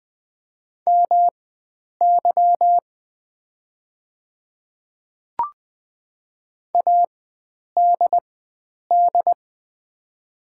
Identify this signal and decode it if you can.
Morse code, heard as a keyed tone, about 20 wpm, spaced as for 9 wpm, 700 Hz. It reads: MY ADD
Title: 4x Character Spacing - Top 300 Words - 20wpm